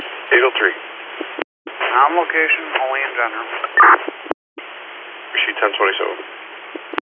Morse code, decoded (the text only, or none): none